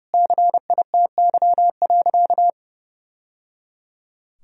Morse code, heard as a keyed tone, about 30 wpm, 700 Hz. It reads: CITY.
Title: Sentences from Top 1000 Words - Review Part 1 - 30wpm